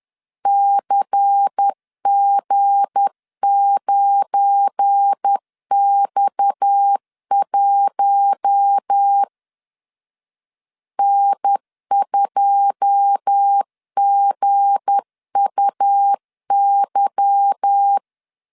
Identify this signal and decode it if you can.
Morse code, heard as a keyed tone, about 10 wpm, 800 Hz. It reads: CG9X1 N2GUY
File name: LEZIONI / 9 2 10wpm 2736